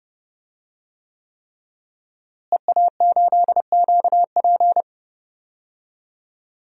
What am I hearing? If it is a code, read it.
Morse code, heard as a keyed tone, about 30 wpm, 700 Hz. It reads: EA8QP
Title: DXCC-100 Call Signs - 5 Characters - Repeated 3x - 30wpm